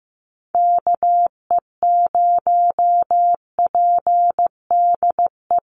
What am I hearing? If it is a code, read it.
Morse code, heard as a keyed tone, about 15 wpm, 700 Hz. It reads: KE0PDE
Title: USA Call Signs 15wpm